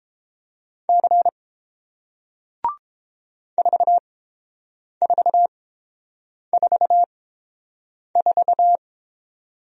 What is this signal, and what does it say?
Morse code, heard as a keyed tone, about 31 wpm, spaced as for 7 wpm, 700 Hz. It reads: C 4444